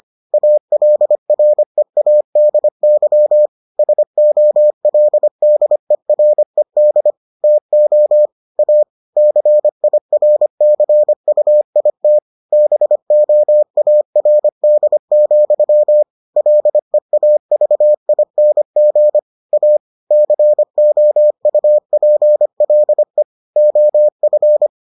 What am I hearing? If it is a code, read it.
Morse code, heard as a keyed tone, about 25 wpm, 600 Hz. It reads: ALREADY SOLDERED TO A CIRCUIT BOARD, LEAVING A COUPLE OF